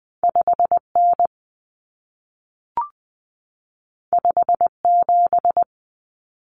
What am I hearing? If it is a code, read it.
Morse code, heard as a keyed tone, about 20 wpm, 700 Hz. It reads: 5N 57
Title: Common Cut Numbers - 2-Digit 20wpm